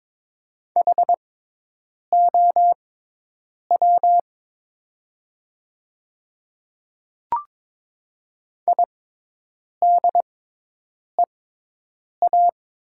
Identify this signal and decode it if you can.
Morse code, heard as a keyed tone, about 22 wpm, spaced as for 8 wpm, 700 Hz. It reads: HOW IDEA
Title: 6x Character Spacing - Top 500 Words - 22wpm